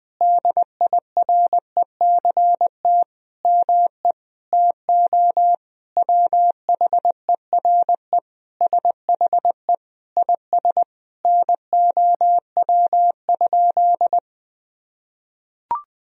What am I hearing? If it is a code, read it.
Morse code, heard as a keyed tone, about 20 wpm, 700 Hz. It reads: DIRECT ME TO WHERE SHE IS NOW?
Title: Sentences from Top 1000 Words - Review Part 2 - 20wpm